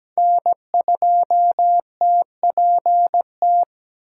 Morse code, heard as a keyed tone, about 17 wpm, 700 Hz. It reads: N2TPT
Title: USA Call Signs - 5 Characters - 2x Character Spacing - 17wpm